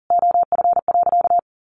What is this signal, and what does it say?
Morse code, heard as a keyed tone, about 40 wpm, 700 Hz. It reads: OF.
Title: Sentences from Top 1200 Words 40wpm